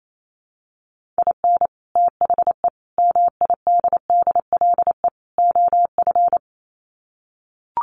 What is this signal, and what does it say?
Morse code, heard as a keyed tone, about 28 wpm, 700 Hz. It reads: IN THE MIDDLE OF
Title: Sets of 4 Words 28wpm